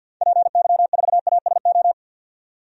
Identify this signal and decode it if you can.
Morse code, heard as a keyed tone, about 45 wpm, 700 Hz. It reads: PY4ASO
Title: DXCC-100 Call Signs - 6 Characters - 50wpm